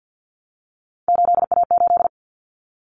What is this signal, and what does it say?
Morse code, heard as a keyed tone, about 50 wpm, 700 Hz. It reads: 8A8
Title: Common Cut Numbers - 3-Digit 50wpm